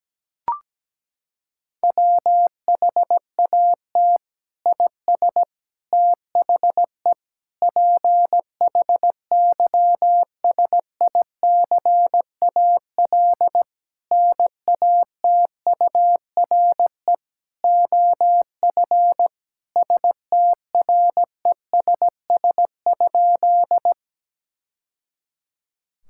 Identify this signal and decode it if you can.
Morse code, heard as a keyed tone, about 17 wpm, 700 Hz. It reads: WHAT IS THE PHYSICAL NATURE OF STRESS?
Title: Sentences from Top 1000 Words - Review Part 6 - 17wpm